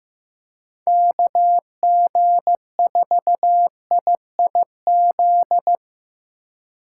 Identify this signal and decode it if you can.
Morse code, heard as a keyed tone, about 15 wpm, 700 Hz. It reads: KG4IIZ
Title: USA Call Signs - 6 Characters - 4x Character Spacing - 15wpm